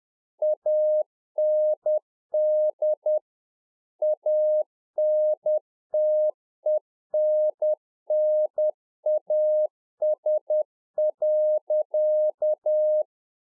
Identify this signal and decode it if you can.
Morse code, heard as a keyed tone, about 10 wpm, 610 Hz. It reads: AND ANTENNAS.